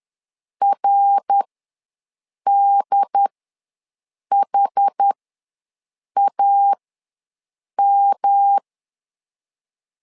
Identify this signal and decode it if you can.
Morse code, heard as a keyed tone, about 11 wpm, spaced as for 6 wpm, 800 Hz. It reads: RDHAM
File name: LEZIONI / 3 MWRO / 3 MWR ws 10wpm 2151